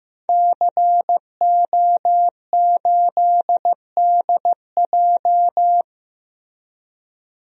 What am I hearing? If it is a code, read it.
Morse code, heard as a keyed tone, about 15 wpm, 700 Hz. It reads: CO8DJ
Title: DXCC-100 Call Signs - 5 Characters - Repeated 3x - 15wpm